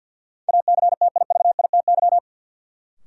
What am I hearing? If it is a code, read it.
Morse code, heard as a keyed tone, about 52 wpm, 700 Hz. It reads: ACTIVITY
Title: Sentences from Top 1000 Words - Review Part 4 - New Word Drill - 50wpm